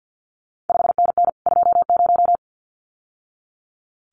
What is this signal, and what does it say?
Morse code, heard as a keyed tone, about 50 wpm, 700 Hz. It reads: 5NN 20
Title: CQ WW DX Contest Exchange - 50wpm